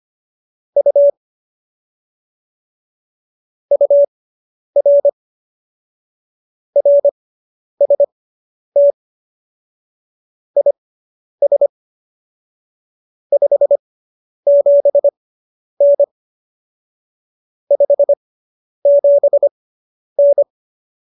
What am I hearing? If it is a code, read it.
Morse code, heard as a keyed tone, about 25 wpm, spaced as for 10 wpm, 570 Hz. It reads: U UR RST IS 57N 57N